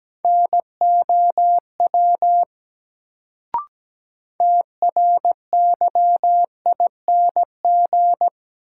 Morse code, heard as a keyed tone, about 17 wpm, 700 Hz. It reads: NOW TRYING
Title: Sets of 2 Words - New Word Drill - 17wpm